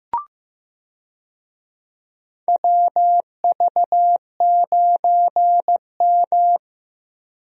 Morse code, heard as a keyed tone, about 15 wpm, 700 Hz. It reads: WV9M
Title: USA Call Signs - 4 Characters - Repeated 2x - 15wpm